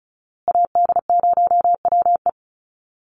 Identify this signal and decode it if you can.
Morse code, heard as a keyed tone, about 34 wpm, 700 Hz. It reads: AD0WE
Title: Call Signs Repeated 2x - 35wpm